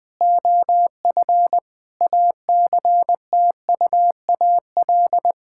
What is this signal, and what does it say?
Morse code, heard as a keyed tone, about 20 wpm, 700 Hz. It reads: OF ACTUAL
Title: Sentences from Top 1000 Words - Review Part 5 - 20wpm